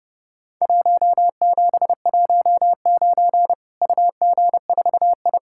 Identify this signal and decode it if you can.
Morse code, heard as a keyed tone, about 30 wpm, 700 Hz. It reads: 1719 UG4I
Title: Worked All Europe DX Contest - QTC Exchange - 30wpm